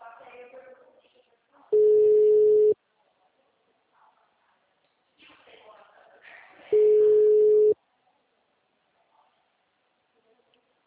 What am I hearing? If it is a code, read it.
Ringback tone (Europe, 425 Hz), with speech.
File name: out- -2002- - - .26788